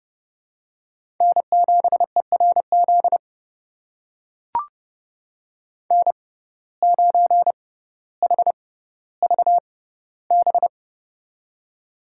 Morse code, heard as a keyed tone, about 32 wpm, spaced as for 16 wpm, 700 Hz. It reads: N7ERZ N9HVB
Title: USA Call Signs - 6x Character Spacing - 30wpm